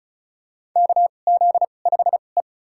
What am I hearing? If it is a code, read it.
Morse code, heard as a keyed tone, about 34 wpm, spaced as for 25 wpm, 700 Hz. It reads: KZ5E
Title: USA Call Signs - 4 Characters - 2x Character Spacing - 35wpm